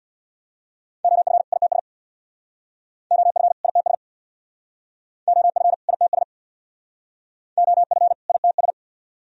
Morse code, heard as a keyed tone, about 62 wpm, 700 Hz. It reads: OF ITS OF ITS OF ITS OF ITS
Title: Speed-Racing - Sets of 2 Words 50wpm